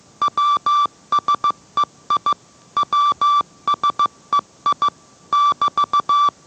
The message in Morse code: WSEI WSEI <BT>

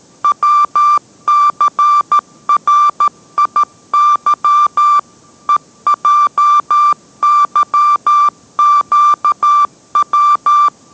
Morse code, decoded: WCRIY EJYQW

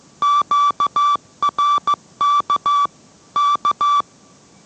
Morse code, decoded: QRK K

